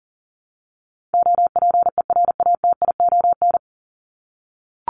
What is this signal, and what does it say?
Morse code, heard as a keyed tone, about 40 wpm, 700 Hz. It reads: OPERATION
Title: Top 1501-1750 Words 40wpm